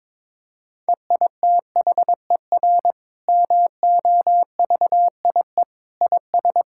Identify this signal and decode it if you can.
Morse code, heard as a keyed tone, about 22 wpm, 700 Hz. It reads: EITHER MOVIE IS
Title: Sentences from Top 1000 Words - Review Part 5 - 22wpm